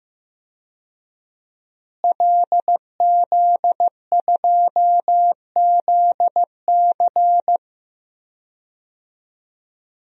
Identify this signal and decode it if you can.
Morse code, heard as a keyed tone, about 16 wpm, 700 Hz. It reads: LZ2ZC
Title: DXCC-100 Call Signs - 5 Characters - Repeated 2x - 15wpm